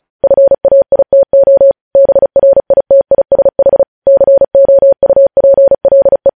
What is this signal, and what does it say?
Morse code, heard as a keyed tone, about 36 wpm, 550 Hz. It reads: FAITO BRITISH COUPLE
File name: podcast / News headlines-1x-35x35